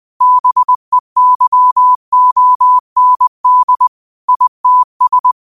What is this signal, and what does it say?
Morse code, heard as a keyed tone, about 20 wpm, 1000 Hz. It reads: BEYOND ITS
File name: qotdfiles / 20wpm